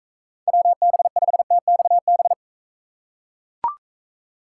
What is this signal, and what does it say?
Morse code, heard as a keyed tone, about 39 wpm, 700 Hz. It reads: WB5TXB